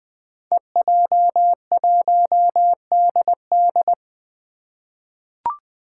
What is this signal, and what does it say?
Morse code, heard as a keyed tone, about 20 wpm, 700 Hz. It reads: EJ1DD
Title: Five Challenging Letter-Number Combinations 20wpm